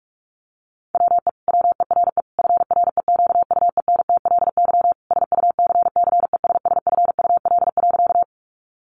Morse code, heard as a keyed tone, about 44 wpm, 700 Hz. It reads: WE WERE FREQUENTLY SUCCESSFUL.
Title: Sentences from Top 1300 Words 45wpm